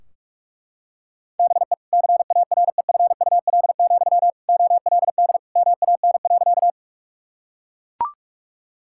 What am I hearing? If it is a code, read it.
Morse code, heard as a keyed tone, about 41 wpm, 700 Hz. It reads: BE CAREFUL, OLD MAN.
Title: Sentences from Top 1600 Words 45wpm